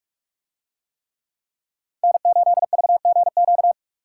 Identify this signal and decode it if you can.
Morse code, heard as a keyed tone, about 46 wpm, 700 Hz. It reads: N8VGQ